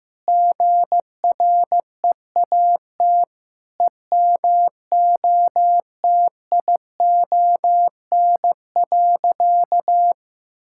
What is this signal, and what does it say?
Morse code, heard as a keyed tone, about 16 wpm, 700 Hz. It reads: GREAT EMOTION.
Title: Sentences from Top 1600 Words 15wpm